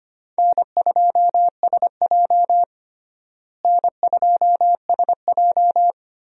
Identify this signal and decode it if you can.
Morse code, heard as a keyed tone, about 25 wpm, 700 Hz. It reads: N2SJ N2SJ
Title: USA Call Signs - 4 Characters - Repeated 2x - 25wpm